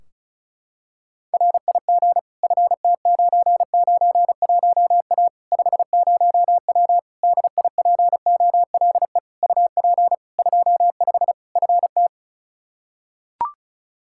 Morse code, heard as a keyed tone, about 34 wpm, 700 Hz. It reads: RIG FT991A 50W DIPOLE UP 25 FT
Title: Practice QSO - Exchange 2 - Info - 35wpm